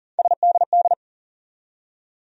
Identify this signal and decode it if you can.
Morse code, heard as a keyed tone, about 39 wpm, 700 Hz. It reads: SDD
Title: USA Call Sign Postfixes - 3 Letters 40wpm